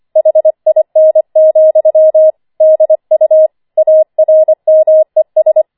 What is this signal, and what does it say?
Morse code, heard as a keyed tone, about 25 wpm, 630 Hz. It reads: HIN, DU ARMES
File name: 25wpm